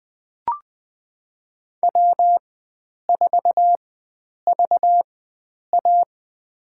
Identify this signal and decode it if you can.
Morse code, heard as a keyed tone, about 21 wpm, spaced as for 9 wpm, 700 Hz. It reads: W4VA
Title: USA Call Signs - 4 Characters - 4x Character Spacing - 20wpm